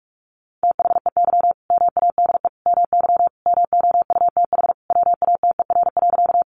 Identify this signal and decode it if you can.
Morse code, heard as a keyed tone, about 44 wpm, 700 Hz. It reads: THEY MADE MY MOUTH WATER.
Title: Sentences from Top 1500 Words 45wpm